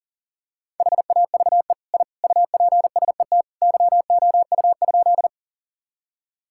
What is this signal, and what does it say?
Morse code, heard as a keyed tone, about 37 wpm, 700 Hz. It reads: HAVE I UPSET YOU?
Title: Sentences from Top 1000 Words - Review Part 5 - 40wpm